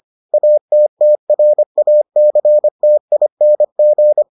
Morse code, heard as a keyed tone, about 25 wpm, 600 Hz. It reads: ATTRACTING